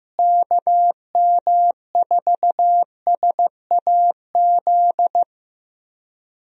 Morse code, heard as a keyed tone, about 15 wpm, 700 Hz. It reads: KM4SAZ
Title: USA Call Signs - 6 Characters - Repeated 2x - 15wpm